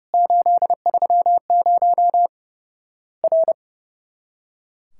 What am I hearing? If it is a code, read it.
Morse code, heard as a keyed tone, about 30 wpm, 700 Hz. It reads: 830 R
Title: Worked All Europe DX Contest - QTC Exchange - 30wpm